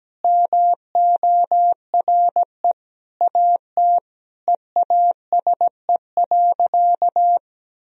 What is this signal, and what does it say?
Morse code, heard as a keyed tone, about 17 wpm, 700 Hz. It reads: MORE AT EASE.
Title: Sentences from Top 1000 Words - Review Part 1 - 17wpm